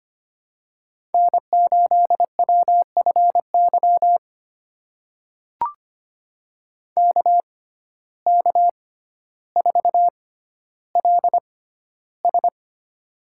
Morse code, heard as a keyed tone, about 25 wpm, spaced as for 13 wpm, 700 Hz. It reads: N8WFY KK4LS
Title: USA Call Signs - 5 Characters - 6x Character Spacing - 25wpm